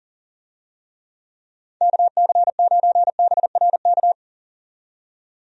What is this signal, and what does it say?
Morse code, heard as a keyed tone, about 40 wpm, 700 Hz. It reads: KC9BRK
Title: USA Call Signs - 6 Characters - 40wpm